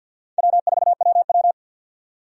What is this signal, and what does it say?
Morse code, heard as a keyed tone, about 53 wpm, 700 Hz. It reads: W4WW
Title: USA Call Signs - 4 Characters - 50wpm